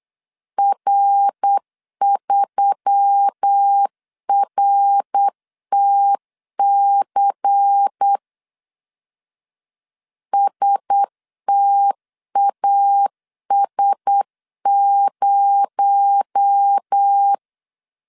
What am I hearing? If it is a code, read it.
Morse code, heard as a keyed tone, about 8 wpm, 800 Hz. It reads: R3RTC STAS0
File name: CW / DET / 1 9 1500C S A 8wpm 5637